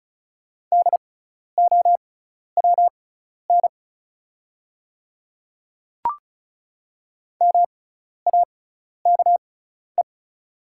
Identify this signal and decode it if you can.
Morse code, heard as a keyed tone, about 36 wpm, spaced as for 12 wpm, 700 Hz. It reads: DOWN MAKE
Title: 6x Character Spacing - Top 100 Words - 35wpm